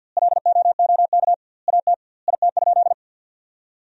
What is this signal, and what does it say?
Morse code, heard as a keyed tone, about 50 wpm, 700 Hz. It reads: LOOK AT IT?
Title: 2x Word-Spacing - Sentences from Top 100 Words - 50wpm